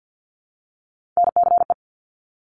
Morse code, heard as a keyed tone, about 50 wpm, 700 Hz. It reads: NCE
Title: Top Three Letter Combinations in Words 50wpm